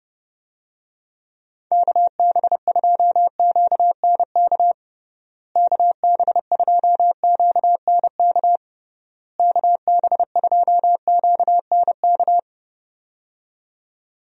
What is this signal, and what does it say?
Morse code, heard as a keyed tone, about 30 wpm, 700 Hz. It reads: KB2QNK KB2QNK KB2QNK
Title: USA Call Signs - 6 Characters - Repeated 3x - 30wpm